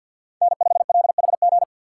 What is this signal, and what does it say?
Morse code, heard as a keyed tone, about 52 wpm, 700 Hz. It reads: N5LHG
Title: USA Call Signs - 5 Characters - 6x Character Spacing - 50wpm